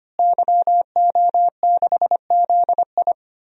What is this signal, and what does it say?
Morse code, heard as a keyed tone, about 25 wpm, 700 Hz. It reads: YO6ZI
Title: DXCC-100 Call Signs - 5 Characters - Repeated 2x - 25wpm